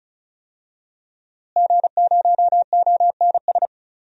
Morse code, heard as a keyed tone, about 36 wpm, 700 Hz. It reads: G0ONS
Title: DXCC-100 Call Signs - 5 Characters - Repeated 3x - 35wpm